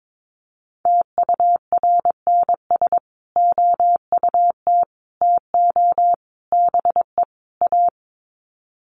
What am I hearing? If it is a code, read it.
Morse code, heard as a keyed tone, about 22 wpm, 700 Hz. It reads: TURNS OUT TO BE A
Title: Sets of 5 Words Encore 22wpm